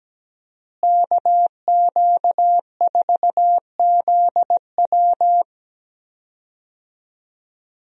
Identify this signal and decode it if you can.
Morse code, heard as a keyed tone, about 17 wpm, 700 Hz. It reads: KQ4ZW
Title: USA Call Signs - 5 Characters - Repeated 2x - 17wpm